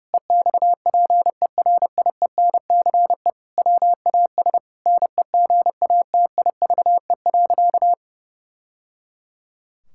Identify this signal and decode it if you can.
Morse code, heard as a keyed tone, about 30 wpm, 700 Hz. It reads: EXPERIENCE WAS NEGATIVE.
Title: Sentences from Top 1000 Words - Review Part 6 - 30wpm